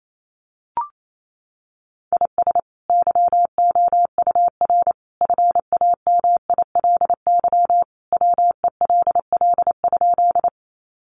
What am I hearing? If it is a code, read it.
Morse code, heard as a keyed tone, about 28 wpm, 700 Hz. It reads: IS YOUR FAMILY WELL?
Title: Sentences from Top 400 Words 28wpm